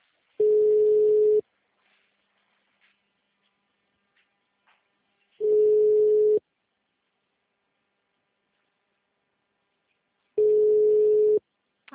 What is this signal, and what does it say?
Ringback tone (Europe, 425 Hz).
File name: out- -2027- - - .15029